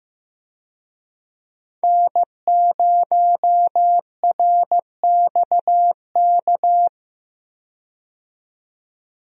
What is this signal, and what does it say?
Morse code, heard as a keyed tone, about 15 wpm, 700 Hz. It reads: N0RXK
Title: USA Call Signs - 5 Characters - 15wpm